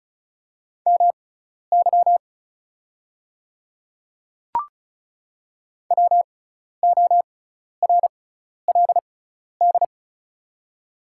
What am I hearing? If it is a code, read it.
Morse code, heard as a keyed tone, about 35 wpm, spaced as for 12 wpm, 690 Hz. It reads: MY WORLD
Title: 6x Character Spacing - Top 200 Words - 35wpm